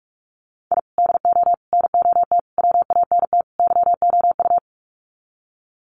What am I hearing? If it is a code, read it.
Morse code, heard as a keyed tone, about 45 wpm, 700 Hz. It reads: I DO NOT WANT YOU
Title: Sets of 5 Words 45wpm